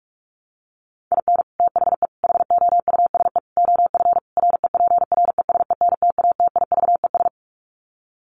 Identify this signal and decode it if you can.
Morse code, heard as a keyed tone, about 44 wpm, 700 Hz. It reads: IN THE HOUSE OF REPRESENTATIVES
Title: Sets of 5 Words Encore 45wpm